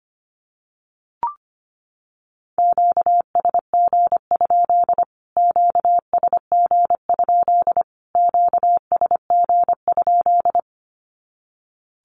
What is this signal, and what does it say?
Morse code, heard as a keyed tone, about 25 wpm, 700 Hz. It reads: QSG? QSG? QSG?
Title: Q-Codes Repeated 3x 25wpm